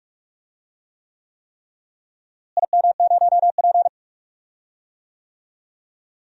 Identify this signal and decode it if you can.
Morse code, heard as a keyed tone, about 41 wpm, 700 Hz. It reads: IM0P
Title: DXCC-100 Call Signs - 4 Characters - Repeated 3x - 45wpm